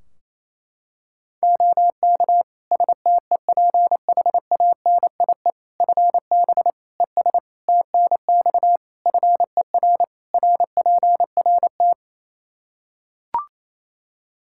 Morse code, heard as a keyed tone, about 28 wpm, 700 Hz. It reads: OK STEPHANIE FB ES TNX FER RPRT
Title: Practice QSO - Exchange 2 - Info - 28wpm